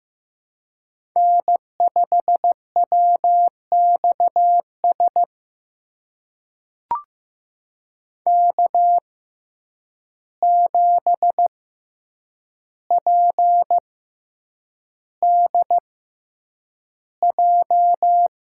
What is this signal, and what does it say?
Morse code, heard as a keyed tone, about 15 wpm, spaced as for 8 wpm, 700 Hz. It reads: N5WXS K7PDJ
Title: USA Call Signs - 5 Characters - 6x Character Spacing - 15wpm